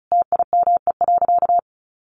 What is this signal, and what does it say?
Morse code, heard as a keyed tone, about 34 wpm, 700 Hz. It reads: TIME.
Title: Sentences from Top 1000 Words 35wpm